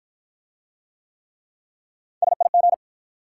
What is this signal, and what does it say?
Morse code, heard as a keyed tone, about 53 wpm, 700 Hz. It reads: SIG